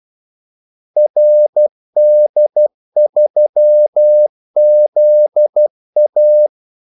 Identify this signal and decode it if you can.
Morse code, heard as a keyed tone, about 12 wpm, 600 Hz. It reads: RD3ZA
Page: 2 ekstra ordmellemrum mellem kaldesignalerne